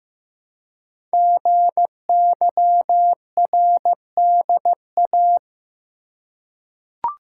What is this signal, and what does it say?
Morse code, heard as a keyed tone, about 15 wpm, 700 Hz. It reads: GYRDA
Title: Code Groups - 5 Letters - 15wpm